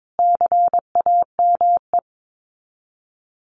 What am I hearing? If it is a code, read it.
Morse code, heard as a keyed tone, about 22 wpm, 700 Hz. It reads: CAME
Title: Top 200 Words Repeated 22wpm